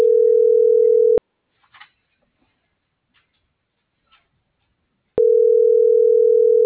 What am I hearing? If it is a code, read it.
Ringback tone (North America), 440 + 480 Hz.